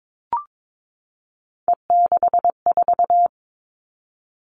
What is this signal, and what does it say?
Morse code, heard as a keyed tone, about 22 wpm, 700 Hz. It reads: E64
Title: Three Challenging Letter-Number Combinations 22wpm